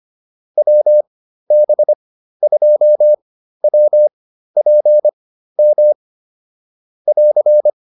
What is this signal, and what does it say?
Morse code, heard as a keyed tone, about 25 wpm, spaced as for 13 wpm, 600 Hz. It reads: WB2WPM <AR>
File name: QSO106 13